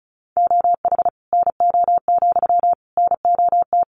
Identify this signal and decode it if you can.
Morse code, heard as a keyed tone, about 34 wpm, 700 Hz. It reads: OH NO, NOT
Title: Sentences from Top 1300 Words 35wpm